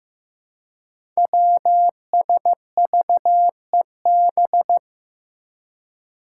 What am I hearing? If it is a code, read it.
Morse code, heard as a keyed tone, about 15 wpm, 700 Hz. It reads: WSVEB
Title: Five Challenging Letter-Number Combinations 15wpm